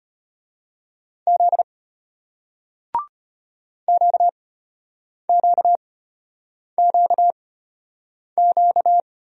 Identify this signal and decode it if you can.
Morse code, heard as a keyed tone, about 33 wpm, spaced as for 8 wpm, 700 Hz. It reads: Z QQQQ